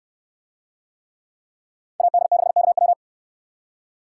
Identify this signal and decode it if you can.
Morse code, heard as a keyed tone, about 69 wpm, 700 Hz. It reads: AD6KW